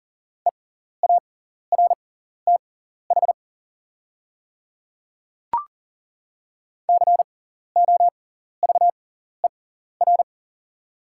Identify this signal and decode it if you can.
Morse code, heard as a keyed tone, about 40 wpm, spaced as for 14 wpm, 690 Hz. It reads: EARTH COVER
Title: 6x Character Spacing - Top 300 Words - 40wpm